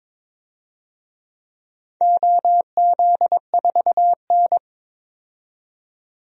Morse code, heard as a keyed tone, about 22 wpm, 700 Hz. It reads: OZ4N
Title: DXCC-100 Call Signs - 4 Characters - 22wpm